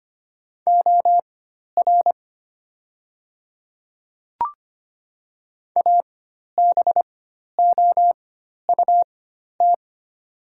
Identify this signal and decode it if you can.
Morse code, heard as a keyed tone, about 25 wpm, spaced as for 12 wpm, 700 Hz. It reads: OR ABOUT
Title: 4x Character Spacing - Top 100 Words - 25wpm